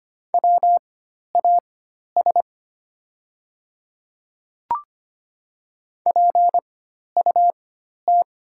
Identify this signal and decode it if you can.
Morse code, heard as a keyed tone, about 25 wpm, spaced as for 12 wpm, 700 Hz. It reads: WAS PUT
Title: 4x Character Spacing - Top 200 Words - 25wpm